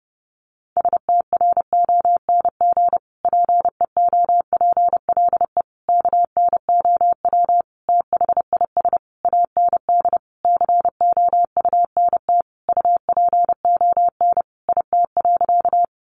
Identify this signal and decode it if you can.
Morse code, heard as a keyed tone, about 30 wpm, 700 Hz. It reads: STRONG PEOPLE KNOW THIS AND COUNT UPON IT.
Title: Sentences from Top 1500 Words 30wpm